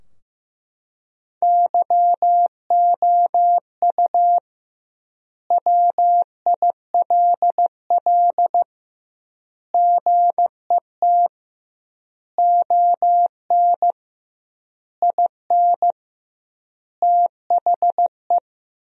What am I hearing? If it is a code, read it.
Morse code, heard as a keyed tone, about 15 wpm, 700 Hz. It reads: YOU WILL GET ON IN THE
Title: 2x Word-Spacing - Sentences from Top 200 Words - 15wpm